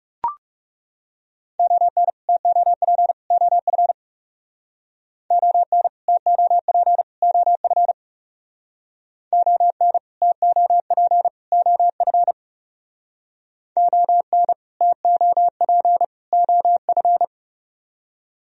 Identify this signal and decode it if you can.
Morse code, heard as a keyed tone, about 37 wpm, 700 Hz. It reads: ON TOP OF ON TOP OF ON TOP OF ON TOP OF
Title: Speed-Racing - Sets of 3 Words 30wpm